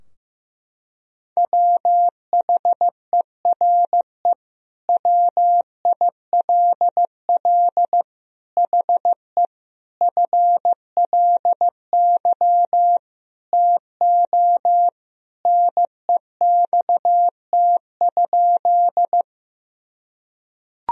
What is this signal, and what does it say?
Morse code, heard as a keyed tone, about 15 wpm, 700 Hz. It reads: WHERE WILL HE FLY TO NEXT?
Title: Sentences from Top 1000 Words - Review Part 2 - 15wpm